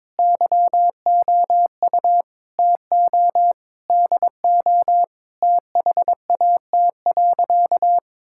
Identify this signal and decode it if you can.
Morse code, heard as a keyed tone, about 22 wpm, 700 Hz. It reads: YOU TO DO THAT.
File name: practice / 22wpm / Sentences with 3-Word Phrases - 22wpm